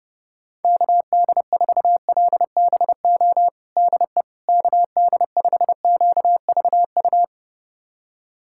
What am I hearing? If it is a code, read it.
Morse code, heard as a keyed tone, about 30 wpm, 700 Hz. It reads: KD4LBO DE KD5QVU
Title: USA Callsign DE Callsign - 6 Characters - 30wpm